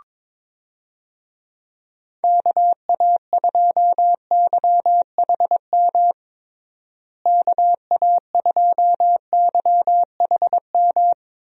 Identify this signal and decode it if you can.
Morse code, heard as a keyed tone, about 22 wpm, 700 Hz. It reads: KA2YHM KA2YHM
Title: USA Call Signs - 6 Characters - Repeated 3x - 22wpm